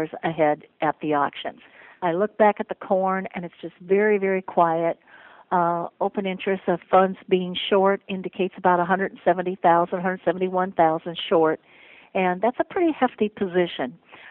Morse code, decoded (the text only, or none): none